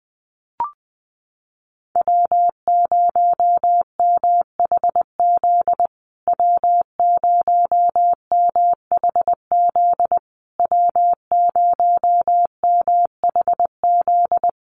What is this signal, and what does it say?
Morse code, heard as a keyed tone, about 20 wpm, 700 Hz. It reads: W0MHZ W0MHZ W0MHZ
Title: USA Call Signs Repeated 3x 20wpm